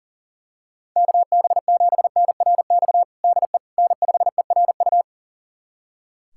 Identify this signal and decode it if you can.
Morse code, heard as a keyed tone, about 40 wpm, 700 Hz. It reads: KB7NRX DE N5ERU
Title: USA Callsign DE Callsign 40wpm